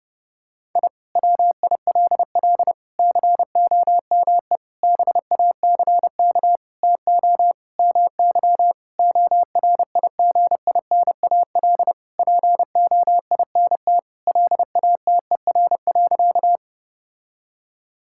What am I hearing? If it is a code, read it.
Morse code, heard as a keyed tone, about 30 wpm, 700 Hz. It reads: I WILL COME BACK TO MY ORIGINAL POINT LATER.
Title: Sentences from Top 1000 Words - Review Part 6 - 30wpm